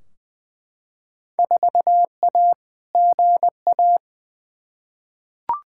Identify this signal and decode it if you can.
Morse code, heard as a keyed tone, about 20 wpm, 700 Hz. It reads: 4A GA
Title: Field Day Exchanges - 20wpm